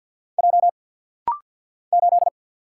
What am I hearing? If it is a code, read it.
Morse code, heard as a keyed tone, about 50 wpm, 690 Hz. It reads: J 8
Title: Single Letter-Number Alt - Rapid-Fire 50wpm